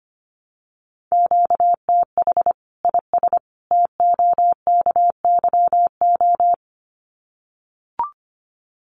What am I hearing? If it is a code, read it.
Morse code, heard as a keyed tone, about 25 wpm, 700 Hz. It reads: QTH IS TOKYO
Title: Common QSO Phrases 25wpm